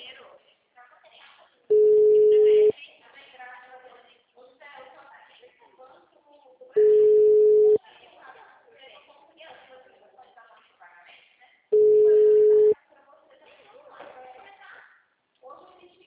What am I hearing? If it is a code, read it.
Ringback tone (Europe, 425 Hz), with speech.
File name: out- -2003- - - .4193